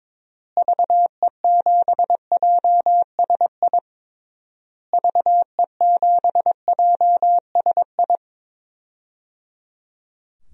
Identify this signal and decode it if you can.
Morse code, heard as a keyed tone, about 22 wpm, 700 Hz. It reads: VE7JSI VE7JSI
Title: DXCC-100 Call Signs - 6 Characters - Repeated 3x - 22wpm